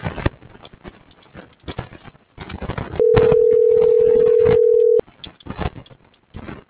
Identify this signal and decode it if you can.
Ringback tone (North America), 440 + 480 Hz.